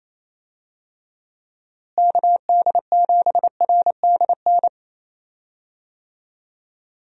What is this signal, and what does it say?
Morse code, heard as a keyed tone, about 27 wpm, 700 Hz. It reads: KD7RDN
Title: USA Call Signs - 6 Characters - Repeated 3x - 28wpm